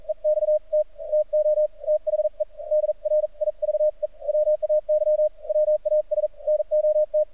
Morse code, heard as a keyed tone, about 43 wpm, spaced as for 36 wpm, 610 Hz. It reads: EXT TO THE DRIVEWAY WAS NOT
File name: archived / CRSnet